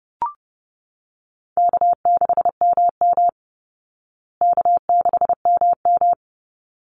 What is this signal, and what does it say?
Morse code, heard as a keyed tone, about 29 wpm, 700 Hz. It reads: K6MM K6MM
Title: Easy Call Signs Repeated 2x - 30wpm